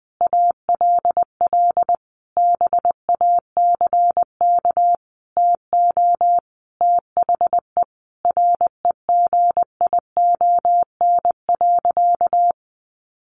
Morse code, heard as a keyed tone, about 20 wpm, 700 Hz. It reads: ALL BACK TO THE REGION.
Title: Sentences from Top 1500 Words 20wpm